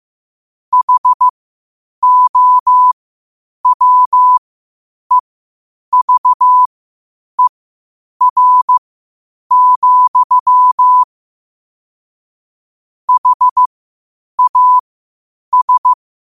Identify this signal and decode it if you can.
Morse code, heard as a keyed tone, about 15 wpm, spaced as for 8 wpm, 1000 Hz. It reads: HOWEVER, HAS
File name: qotdfiles / 8wpm